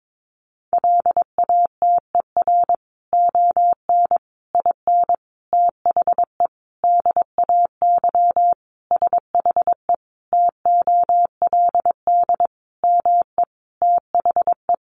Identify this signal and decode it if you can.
Morse code, heard as a keyed tone, about 22 wpm, 700 Hz. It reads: LATER ON IN THE DAY SHE TOLD ME THE
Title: Sentences from Top 700 Words 22wpm